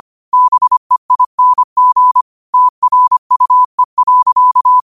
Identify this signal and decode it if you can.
Morse code, heard as a keyed tone, about 25 wpm, 1000 Hz. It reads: BEING TRUE.